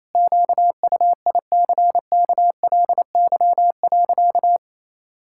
Morse code, heard as a keyed tone, about 28 wpm, 700 Hz. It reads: QUICKLY.
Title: Pangram Challenge v2 - 28wpm